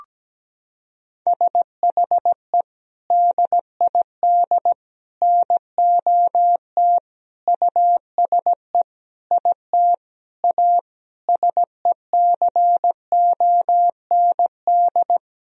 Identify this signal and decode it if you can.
Morse code, heard as a keyed tone, about 17 wpm, 700 Hz. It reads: SHE DID NOT USE IT A SECOND
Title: Sentences from Top 1000 Words - Review Part 1 - 17wpm